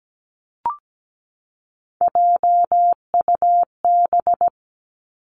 Morse code, heard as a keyed tone, about 17 wpm, 700 Hz. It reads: JUB
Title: Three Challenging Letter-Number Combinations 17wpm